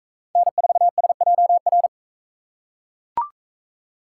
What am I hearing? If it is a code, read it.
Morse code, heard as a keyed tone, about 38 wpm, 700 Hz. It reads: N4SJR